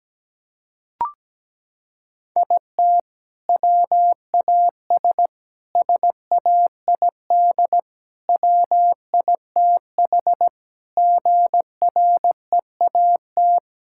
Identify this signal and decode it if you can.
Morse code, heard as a keyed tone, about 17 wpm, 700 Hz. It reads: IT WAS SAID WITH GREAT
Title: Sentences from Top 1600 Words 17wpm